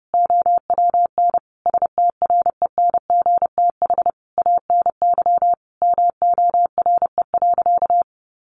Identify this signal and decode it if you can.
Morse code, heard as a keyed tone, about 30 wpm, 700 Hz. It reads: OWN STRENGTH ANY MORE.
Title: Sentences from Top 1300 Words 30wpm